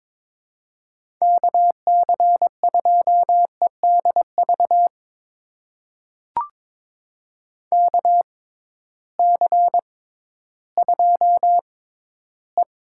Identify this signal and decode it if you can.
Morse code, heard as a keyed tone, about 22 wpm, spaced as for 13 wpm, 700 Hz. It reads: KC2EDV KC2E
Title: USA Call Signs - 6 Characters - 6x Character Spacing - 22wpm